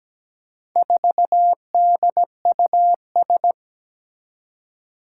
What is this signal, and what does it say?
Morse code, heard as a keyed tone, about 17 wpm, 700 Hz. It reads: 4DUS
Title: Four Challenging Letter-Number Combinations 17wpm